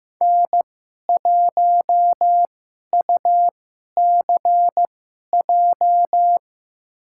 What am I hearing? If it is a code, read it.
Morse code, heard as a keyed tone, about 15 wpm, spaced as for 11 wpm, 700 Hz. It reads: N1UCJ
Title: USA Call Signs - 5 Characters - 2x Character Spacing - 15wpm